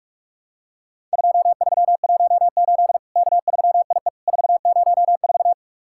Morse code, heard as a keyed tone, about 43 wpm, 700 Hz. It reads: 2318 K3IE 404
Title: Worked All Europe DX Contest - QTC Exchange - 45wpm